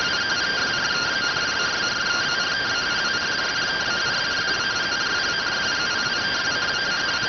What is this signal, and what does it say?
Two-tone teleprinter signal, about 170 Hz shift.